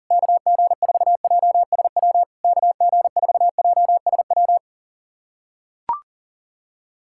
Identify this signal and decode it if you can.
Morse code, heard as a keyed tone, about 40 wpm, 700 Hz. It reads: KG4JSW KG4JSW
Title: SOTA QSO Practice 40wpm